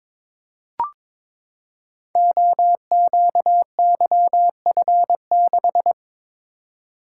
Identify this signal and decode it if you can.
Morse code, heard as a keyed tone, about 22 wpm, 700 Hz. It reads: OQYF6
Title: Code Groups - 5 Letter:Numbers - 22wpm